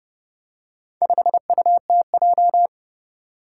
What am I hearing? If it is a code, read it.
Morse code, heard as a keyed tone, about 30 wpm, 700 Hz. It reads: 5UTJ
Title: Four Challenging Letter-Number Combinations 30wpm